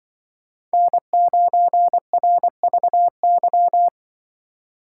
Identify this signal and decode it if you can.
Morse code, heard as a keyed tone, about 24 wpm, 700 Hz. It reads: N9RVY
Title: Speed Racing - USA 1x3 Call Signs 20wpm